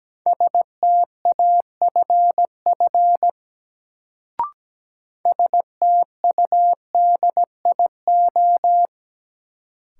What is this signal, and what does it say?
Morse code, heard as a keyed tone, about 17 wpm, 700 Hz. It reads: STAFF STUDIO
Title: Sentences from Top 1200 Words - New Word Drill Alt - 17wpm